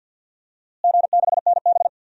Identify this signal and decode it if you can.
Morse code, heard as a keyed tone, about 49 wpm, 700 Hz. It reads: G6NB
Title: DXCC-100 Call Signs - 4 Characters - Repeated 2x - 50wpm